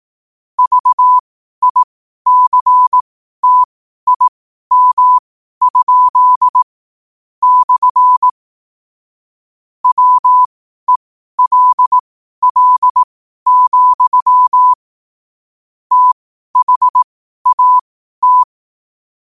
Morse code, heard as a keyed tone, about 18 wpm, spaced as for 12 wpm, 1000 Hz. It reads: VICTIM? / WELL, THAT